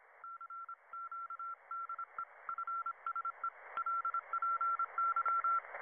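Morse code, heard as a keyed tone, about 26 wpm, 1360 Hz. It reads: CQDE FSE LPY
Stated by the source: Emergency broadcast and ham radio signals in a fictional audio drama after a world-ending eruption.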